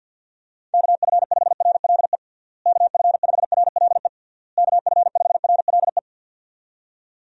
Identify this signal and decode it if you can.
Morse code, heard as a keyed tone, about 48 wpm, 700 Hz. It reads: KF5RLE KF5RLE KF5RLE
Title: USA Call Signs - 6 Characters - Repeated 3x - 50wpm